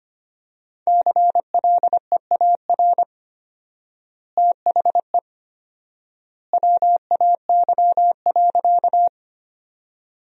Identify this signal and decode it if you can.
Morse code, heard as a keyed tone, about 25 wpm, 700 Hz. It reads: CLEAR THE WAY.
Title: 4x Word-Spacing - Sentences from Top 500 Words - 25wpm